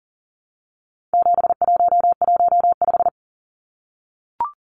Morse code, heard as a keyed tone, about 40 wpm, 700 Hz. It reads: 7115
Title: Numbers - 4-Digit 40wpm